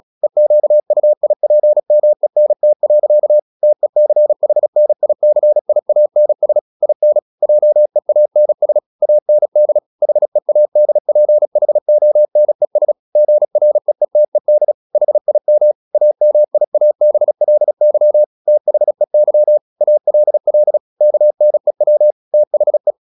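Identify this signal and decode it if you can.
Morse code, heard as a keyed tone, about 36 wpm, 600 Hz. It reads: EQUIPMENT. TECHNICIANS IN JEANS AND HEADPHONES GREETED HIM AMIABLY THEY ALL KNEW THE
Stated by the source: Morse practice at 36 wpm, text of Ken Follett's novel Never, from CW med Gnister.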